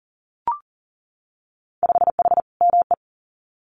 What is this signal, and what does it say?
Morse code, heard as a keyed tone, about 40 wpm, 700 Hz. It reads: 5H ME
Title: Winter Field Day Exchanges Encore 40wpm